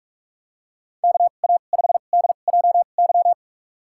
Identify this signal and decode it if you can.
Morse code, heard as a keyed tone, about 41 wpm, spaced as for 31 wpm, 700 Hz. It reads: KA5DJY
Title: USA Call Signs - 6 Characters - 2x Character Spacing - 45wpm